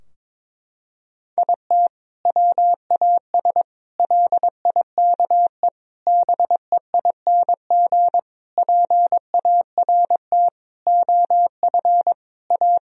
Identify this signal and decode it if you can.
Morse code, heard as a keyed tone, about 22 wpm, 700 Hz. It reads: IT WAS LIKE BEING PART OF A
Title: Sentences from Top 1000 Words - Review Part 3 - 22wpm